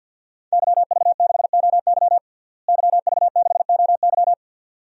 Morse code, heard as a keyed tone, about 52 wpm, 700 Hz. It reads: YV6OY YV6OY
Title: DXCC-100 Call Signs - 5 Characters - Repeated 2x - 50wpm